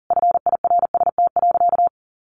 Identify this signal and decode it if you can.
Morse code, heard as a keyed tone, about 40 wpm, 700 Hz. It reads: FIRST.
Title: Sentences from Top 400 Words 40wpm